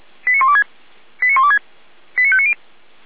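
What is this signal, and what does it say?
Five-tone selective call (ZVEI), heard as quick runs of five tones, tones of about 70 ms. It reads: 89126 89126 89590
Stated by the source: Received on SDR; ZVEI five-tone call 89120, channel 455.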